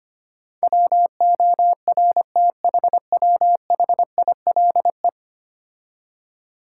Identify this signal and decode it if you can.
Morse code, heard as a keyed tone, about 25 wpm, 700 Hz. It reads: WORTHWHILE
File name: 10 Letter Words - First 3 Letters Given - 25wpm